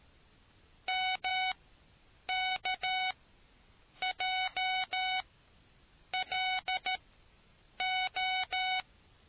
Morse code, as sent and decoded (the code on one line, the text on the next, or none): -- -.- .--- .-.. ---
MKJLO